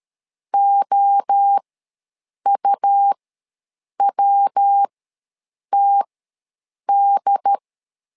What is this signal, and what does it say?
Morse code, heard as a keyed tone, about 13 wpm, spaced as for 7 wpm, 800 Hz. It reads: OUWTD
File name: LEZIONI / 3 MWRO / 3 MWRO ws 12wpm 1914